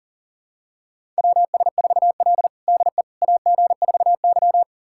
Morse code, heard as a keyed tone, about 40 wpm, 700 Hz. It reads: WS4L DE AG4Y
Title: USA Callsign DE Callsign - 4 Characters - 40wpm